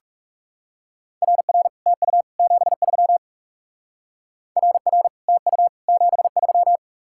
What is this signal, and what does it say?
Morse code, heard as a keyed tone, about 38 wpm, 700 Hz. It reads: RR TU 73 RR TU 73